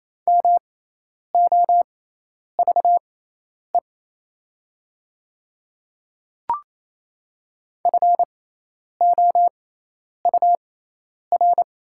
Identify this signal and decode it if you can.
Morse code, heard as a keyed tone, about 27 wpm, spaced as for 10 wpm, 700 Hz. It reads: MOVE FOUR
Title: 6x Character Spacing - Top 300 Words - 28wpm